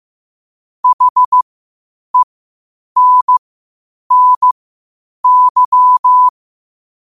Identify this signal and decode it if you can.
Morse code, heard as a keyed tone, about 15 wpm, spaced as for 8 wpm, 1000 Hz. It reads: HENNY
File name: qotdfiles / 8wpm